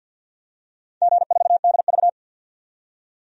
Morse code, heard as a keyed tone, about 50 wpm, 700 Hz. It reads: G4DV
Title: DXCC-100 Call Signs - 4 Characters - 50wpm